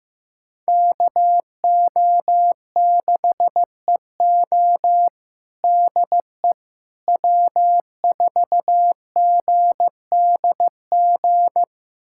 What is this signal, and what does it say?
Morse code, heard as a keyed tone, about 15 wpm, 700 Hz. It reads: KO6EO DE W4GDG
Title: USA Callsign DE Callsign - 5 Characters - 15wpm